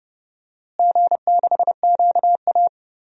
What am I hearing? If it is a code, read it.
Morse code, heard as a keyed tone, about 30 wpm, 700 Hz. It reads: G6QA
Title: DXCC-100 Call Signs - 4 Characters - Repeated 3x - 30wpm